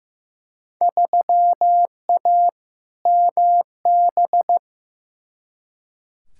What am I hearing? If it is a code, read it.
Morse code, heard as a keyed tone, about 16 wpm, 700 Hz. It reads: 3A MB
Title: Field Day Exchanges Encore - 15wpm